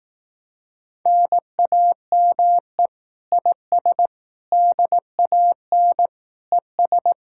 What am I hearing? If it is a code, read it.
Morse code, heard as a keyed tone, about 18 wpm, 700 Hz. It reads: NAME IS DAN ES
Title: Speed-Racing Common QSO Phrases Encore - 15wpm